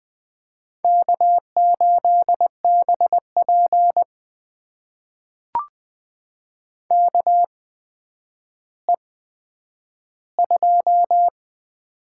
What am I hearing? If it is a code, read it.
Morse code, heard as a keyed tone, about 20 wpm, spaced as for 10 wpm, 700 Hz. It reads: K8BP KE2
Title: USA Call Signs - 8x Character Spacing - 20wpm